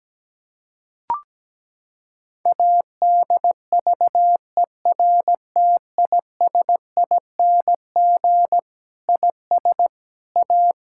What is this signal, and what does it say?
Morse code, heard as a keyed tone, about 17 wpm, 700 Hz. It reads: ADVERTISING IS A
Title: Sentences from Top 1000 Words - Review Part 6 - 17wpm